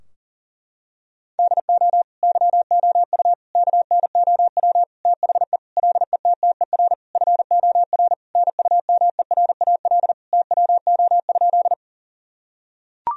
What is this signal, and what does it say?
Morse code, heard as a keyed tone, about 37 wpm, 700 Hz. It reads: DO YOU KNOW THE LETTER FOR NUMERAL TWO?
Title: Sentences from Top 1000 Words - Review Part 2 - 40wpm